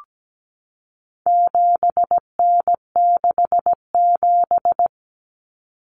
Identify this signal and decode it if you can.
Morse code, heard as a keyed tone, about 17 wpm, 700 Hz. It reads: 7N67
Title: Common Cut Numbers - 4-Digit 17wpm